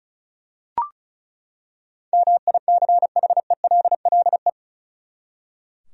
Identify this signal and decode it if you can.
Morse code, heard as a keyed tone, about 34 wpm, 700 Hz. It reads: MICHELLE
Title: USA Common Names - 35wpm